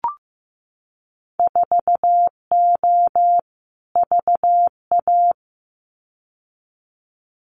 Morse code, heard as a keyed tone, about 15 wpm, 700 Hz. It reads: 4O VA
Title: Winter Field Day Exchanges Encore 15wpm